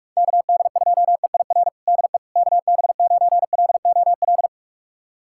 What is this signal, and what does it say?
Morse code, heard as a keyed tone, about 45 wpm, 700 Hz. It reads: KD2EIR DE KB9LOL
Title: USA Callsign DE Callsign - 6 Characters - 45wpm